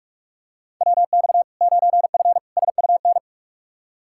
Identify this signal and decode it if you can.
Morse code, heard as a keyed tone, about 41 wpm, 700 Hz. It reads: WX 9F SUN
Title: Practice QSO - Exchange 2 - Info - 45wpm